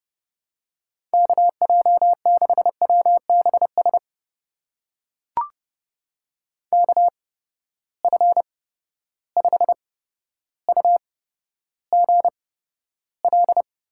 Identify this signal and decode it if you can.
Morse code, heard as a keyed tone, about 30 wpm, spaced as for 13 wpm, 700 Hz. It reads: KJ6WBS KF5UGL
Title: USA Call Signs - 6 Characters - 8x Character Spacing - 30wpm